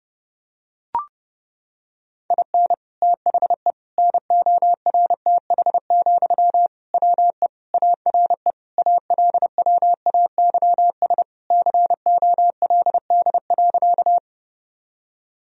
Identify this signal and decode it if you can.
Morse code, heard as a keyed tone, about 32 wpm, 700 Hz. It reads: IN THE NORTH, WE ARE ALWAYS COLD.
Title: Sentences from Top 1000 Words - Review Part 1 - 30wpm